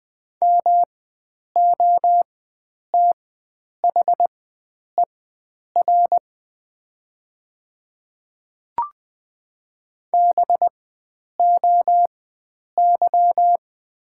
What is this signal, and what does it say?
Morse code, heard as a keyed tone, about 20 wpm, spaced as for 9 wpm, 700 Hz. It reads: MOTHER BOY
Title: 4x Character Spacing - Top 500 Words - 20wpm